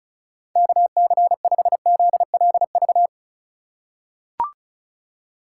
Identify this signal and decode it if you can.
Morse code, heard as a keyed tone, about 36 wpm, 700 Hz. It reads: KC5ZLV
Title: USA Call Signs - 3x Character Spacing - 35wpm